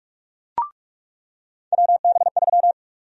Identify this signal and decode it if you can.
Morse code, heard as a keyed tone, about 46 wpm, 700 Hz. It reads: WB3
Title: USA Call Sign Prefixes - 45wpm